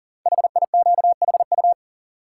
Morse code, heard as a keyed tone, about 41 wpm, 690 Hz. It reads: HIQHU
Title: Code Groups - 5 Letters - 40wpm